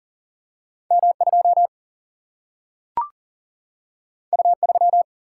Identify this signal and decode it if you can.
Morse code, heard as a keyed tone, about 41 wpm, 700 Hz. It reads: M2 U3
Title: Single-Letter Single-Number 40wpm